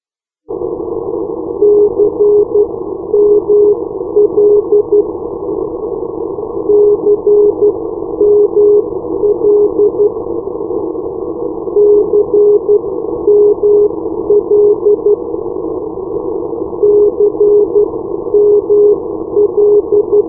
Morse code, heard as a keyed tone, about 12 wpm, spaced as for 10 wpm, 410 Hz. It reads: CML CML CML CML
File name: europe / irl / irl-340-cml-387 (ei2hh at sdr.hu) k4 2019